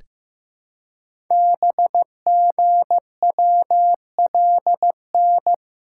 Morse code, heard as a keyed tone, about 15 wpm, 700 Hz. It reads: BGWLN
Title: Code Groups - 5 Letters - 15wpm